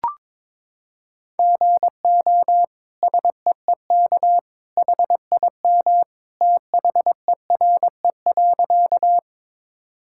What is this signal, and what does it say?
Morse code, heard as a keyed tone, about 21 wpm, 700 Hz. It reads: GO SEEK HIM THERE.
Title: Sentences from Top 1600 Words 22wpm